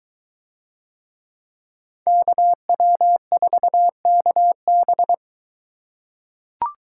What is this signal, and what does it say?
Morse code, heard as a keyed tone, about 21 wpm, 700 Hz. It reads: KW4KB